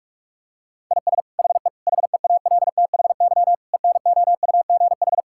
Morse code, heard as a keyed tone, about 47 wpm, 700 Hz. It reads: IS HE HEALTHY ENOUGH
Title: Sentences from Top 1000 Words - Review Part 5 - 45wpm